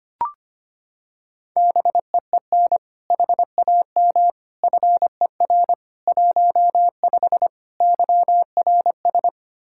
Morse code, heard as a keyed tone, about 25 wpm, 700 Hz. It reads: BEEN HAM FER 15 YRS
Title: Practice QSO - Exchange 3 - More Info - 25wpm